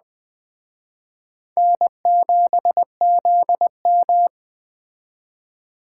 Morse code, heard as a keyed tone, about 20 wpm, 700 Hz. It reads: N7ZM
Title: Speed Racing - USA 1x2 Call Signs 17wpm